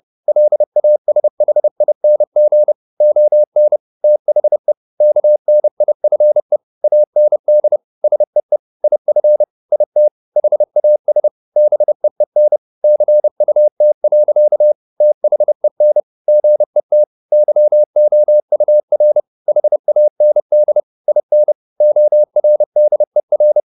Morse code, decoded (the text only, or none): LASHING ON THE KNIFE AND SEE IF IT HAS BEEN CUT. THEN GET YOUR HAND IN ORDER